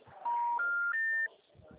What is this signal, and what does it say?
Special information tone.
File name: out- -2004- - - .116